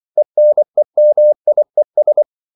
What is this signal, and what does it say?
Morse code, heard as a keyed tone, about 24 wpm, 600 Hz. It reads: ENEMIES
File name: CW / Practice / EnglishQuotations0324